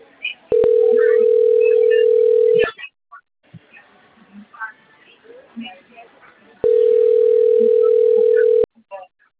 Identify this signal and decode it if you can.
Ringback tone (North America), 440 + 480 Hz, with speech.
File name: out- - - - - .14917